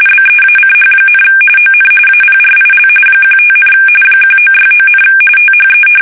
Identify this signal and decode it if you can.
RTTY, about 45.45 baud. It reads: BEACH, ON APRIL 15, 1969, THE
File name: fsk850 / battleshipnj